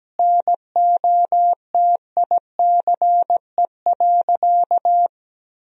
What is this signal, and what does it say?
Morse code, heard as a keyed tone, about 17 wpm, 700 Hz. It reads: NOTICE.
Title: 2x Word-Spacing - Sentences from Top 400 Words - 17wpm